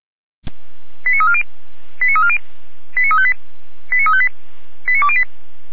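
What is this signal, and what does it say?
Five-tone selective call (ZVEI), heard as quick runs of five tones, tones of about 80 ms. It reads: 89340 89340 89358 89358 89208
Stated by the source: ZVEI five-tone selective call received on an SDR.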